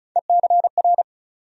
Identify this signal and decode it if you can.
Morse code, heard as a keyed tone, about 36 wpm, 700 Hz. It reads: ECR
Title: USA Call Sign Postfixes - 3 Letters 35wpm